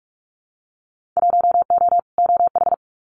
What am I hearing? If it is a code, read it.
Morse code, heard as a keyed tone, about 44 wpm, 700 Hz. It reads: 1O OH
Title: Winter Field Day Exchanges 45wpm